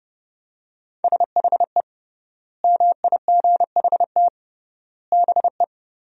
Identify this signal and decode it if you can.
Morse code, heard as a keyed tone, about 30 wpm, 700 Hz. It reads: SHE MIGHT BE
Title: 3x Word-Spacing - Sentences from Top 300 Words - 30wpm